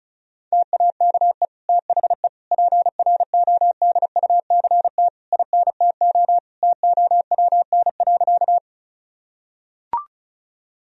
Morse code, heard as a keyed tone, about 34 wpm, 700 Hz. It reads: TAKE THE PRODUCT INTO TOWN.
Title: Sentences from Top 1000 Words - Review Part 2 - 35wpm